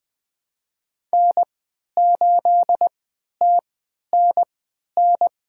Morse code, heard as a keyed tone, about 21 wpm, spaced as for 11 wpm, 700 Hz. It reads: N8TNN
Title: USA Call Signs - 5 Characters - 3x Character Spacing - 20wpm